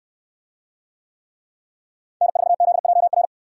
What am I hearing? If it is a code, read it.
Morse code, heard as a keyed tone, about 68 wpm, 700 Hz. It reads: N4ZJR